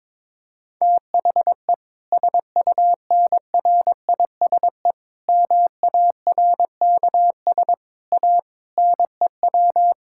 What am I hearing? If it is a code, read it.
Morse code, heard as a keyed tone, about 22 wpm, 700 Hz. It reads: THE SUNRISE MARKS A NEW
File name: Difficult Sentences with Hints - 22wpm